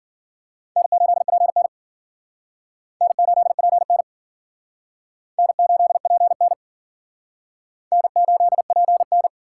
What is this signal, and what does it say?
Morse code, heard as a keyed tone, about 54 wpm, 700 Hz. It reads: N8PN N8PN N8PN N8PN